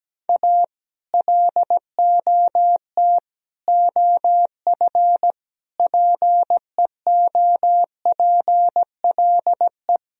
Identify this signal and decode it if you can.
Morse code, heard as a keyed tone, about 17 wpm, 700 Hz. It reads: A LOT OF PEOPLE
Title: Speed-Racing - Sets of 5 Words 17wpm